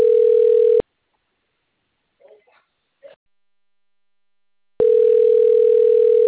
Ringback tone (North America), 440 + 480 Hz.